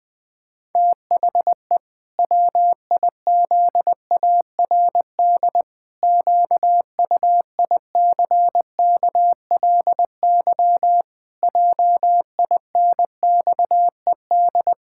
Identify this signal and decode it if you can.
Morse code, heard as a keyed tone, about 20 wpm, 700 Hz. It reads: THE WIZARD QUICKLY JINXED
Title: Pangram Challenge v2 - 20wpm